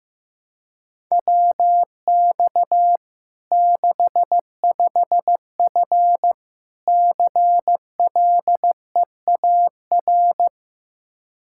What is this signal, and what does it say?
Morse code, heard as a keyed tone, about 15 wpm, 700 Hz. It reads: WX 65F CLEAR
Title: Practice QSO - Exchange 2 - Info - 15wpm